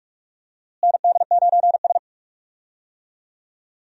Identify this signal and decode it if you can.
Morse code, heard as a keyed tone, about 41 wpm, 700 Hz. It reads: ND9S
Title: USA Call Signs - 4 Characters - Repeated 3x - 45wpm